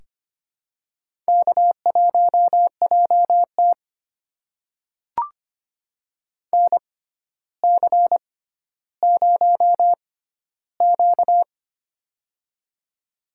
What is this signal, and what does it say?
Morse code, heard as a keyed tone, about 25 wpm, spaced as for 13 wpm, 700 Hz. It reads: K1JT NC0Q
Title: USA Call Signs - 4 Characters - 6x Character Spacing - 25wpm